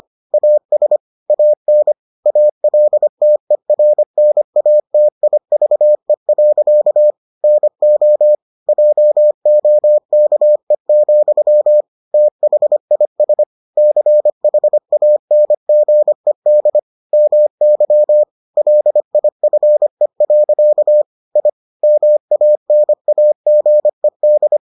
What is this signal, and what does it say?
Morse code, heard as a keyed tone, about 25 wpm, 600 Hz. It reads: AS AN ALTERNATIVE. NO JOKE, THIS CHANGED MY LIFE. I MANAGED